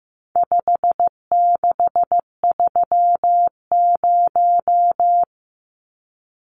Morse code, heard as a keyed tone, about 15 wpm, 700 Hz. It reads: 5630
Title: Numbers - 4-Digit 15wpm